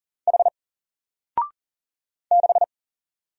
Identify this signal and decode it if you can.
Morse code, heard as a keyed tone, about 40 wpm, 700 Hz. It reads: H 6
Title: Single Letter-Number Alt 40wpm